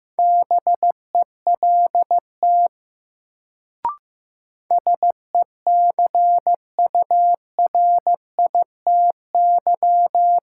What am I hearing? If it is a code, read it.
Morse code, heard as a keyed tone, about 15 wpm, 700 Hz. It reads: BELT SECURITY
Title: Sentences from Top 1400 Words - New Word Drill Alt - 15wpm